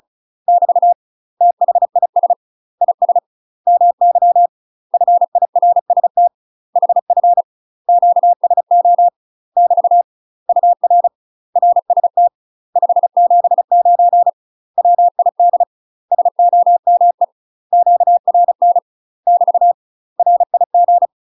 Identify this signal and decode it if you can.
Morse code, heard as a keyed tone, about 34 wpm, 700 Hz. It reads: <BT> THIS IS MY FIRST HF QSO <BT> UR RST 579 WID SOME QRN <BT> RIG
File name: UK MASTER QSOs TRAINING / 35WPM / UK MASTER QSOs 0040